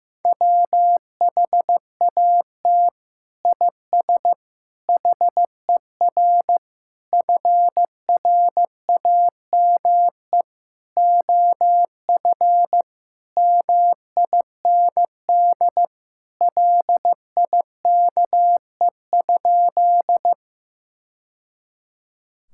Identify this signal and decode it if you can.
Morse code, heard as a keyed tone, about 15 wpm, 700 Hz. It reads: WHAT IS HER FRAME OF MIND LIKE?
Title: Sentences from Top 1000 Words - Review Part 6 - 15wpm